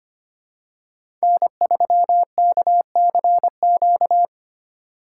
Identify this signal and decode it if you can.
Morse code, heard as a keyed tone, about 25 wpm, 700 Hz. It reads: N3KCQ
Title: USA Call Signs - 5 Characters - 3x Character Spacing - 25wpm